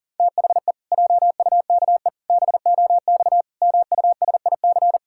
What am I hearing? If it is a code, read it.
Morse code, heard as a keyed tone, about 40 wpm, 700 Hz. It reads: THE JUKE BOX MUSIC
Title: Pangram Challenge v2 - 40wpm